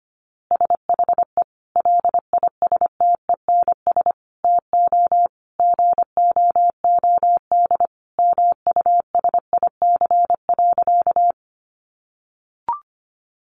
Morse code, heard as a keyed tone, about 25 wpm, 700 Hz. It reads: SHE LISTENS TO GOOD MUSIC.
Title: Easy Phrases 25wpm